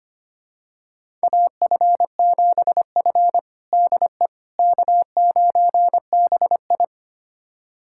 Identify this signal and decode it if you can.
Morse code, heard as a keyed tone, about 25 wpm, 700 Hz. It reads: AF7F DE K9BI
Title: USA Callsign DE Callsign - 4 Characters - 25wpm